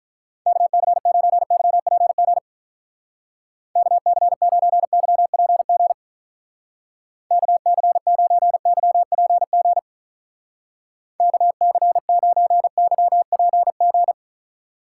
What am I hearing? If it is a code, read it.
Morse code, heard as a keyed tone, about 43 wpm, 700 Hz. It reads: KC9YPG KC9YPG KC9YPG KC9YPG
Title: Speed Racing - USA 2x3 Call Signs 35wpm